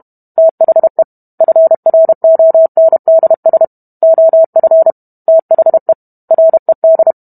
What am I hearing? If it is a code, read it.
Morse code, heard as a keyed tone, about 32 wpm, 650 Hz. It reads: THE FRONDS OF THE RED
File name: Book Two / 05-Chapter 5 - 31 WPM